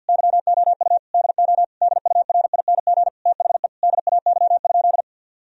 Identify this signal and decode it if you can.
Morse code, heard as a keyed tone, about 50 wpm, 700 Hz. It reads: YOU DO DURING THE DAY?
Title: Sentences from Top 1000 Words - Review Part 2 - 50wpm